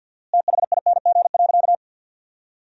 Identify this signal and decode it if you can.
Morse code, heard as a keyed tone, about 50 wpm, 700 Hz. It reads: THING.
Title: 4x Word-Spacing - Sentences from Top 300 Words - 50wpm